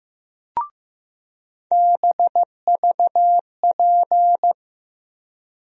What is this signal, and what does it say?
Morse code, heard as a keyed tone, about 15 wpm, 700 Hz. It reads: BVP
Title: USA Call Sign Postfixes - 3 Letters 15wpm